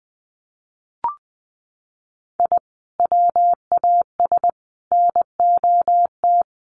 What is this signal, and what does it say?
Morse code, heard as a keyed tone, about 20 wpm, 700 Hz. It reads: I WAS NOT
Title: Sets of 3 Words 20wpm